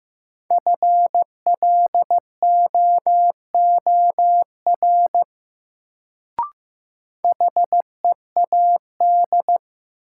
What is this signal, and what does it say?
Morse code, heard as a keyed tone, about 15 wpm, 700 Hz. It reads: FLOOR HEAD
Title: Sets of 3 Words - New Word Drill - 15wpm